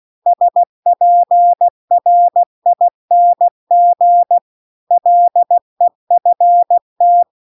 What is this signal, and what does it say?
Morse code, heard as a keyed tone, about 16 wpm, 700 Hz. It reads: SPRING LEFT
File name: CW-The-Big-Twoheaded-River-2-16WPM